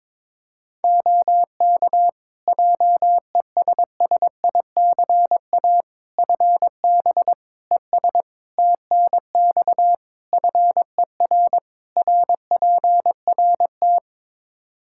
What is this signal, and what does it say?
Morse code, heard as a keyed tone, about 22 wpm, 700 Hz. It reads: OK JESSICA FB ES TNX FER RPRT
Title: Practice QSO - Exchange 2 - Info - 22wpm